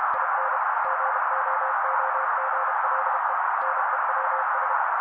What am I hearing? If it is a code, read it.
Morse code, heard as a keyed tone, about 32 wpm, 530 Hz. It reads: IN GOOGLE NEWS
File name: daily / 31wpm noise